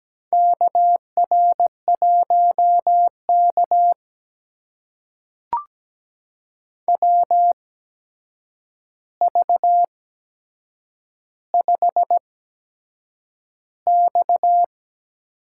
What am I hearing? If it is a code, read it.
Morse code, heard as a keyed tone, about 17 wpm, spaced as for 7 wpm, 700 Hz. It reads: KR1K WV5X
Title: USA Call Signs - 4 Characters - 8x Character Spacing - 17wpm